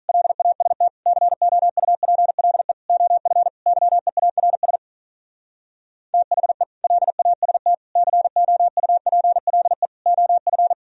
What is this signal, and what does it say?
Morse code, heard as a keyed tone, about 46 wpm, 700 Hz. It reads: LAST COUPLE OF YEARS THE LAST COUPLE OF